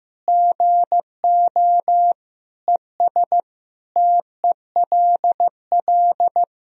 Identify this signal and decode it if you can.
Morse code, heard as a keyed tone, about 15 wpm, 700 Hz. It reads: GO ES TELL
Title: Sentences from Top 200 Words and Abbreviations 15wpm